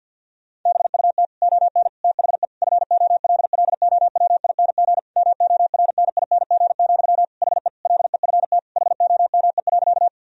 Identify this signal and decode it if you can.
Morse code, heard as a keyed tone, about 50 wpm, 700 Hz. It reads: BUT ON THE FOLLOWING MORNING, HE LEFT HOME.
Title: Sentences from Top 1000 Words - Review Part 4 - 50wpm